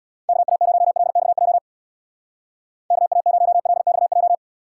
Morse code, heard as a keyed tone, about 61 wpm, 700 Hz. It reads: KA0LXJ KA0LXJ
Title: Speed Racing - USA 2x3 Call Signs 50wpm